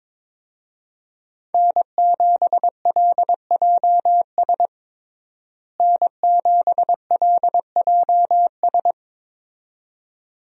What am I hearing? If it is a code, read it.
Morse code, heard as a keyed tone, about 22 wpm, 700 Hz. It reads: N7LJS N7LJS
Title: USA Call Signs - 5 Characters - Repeated 2x - 22wpm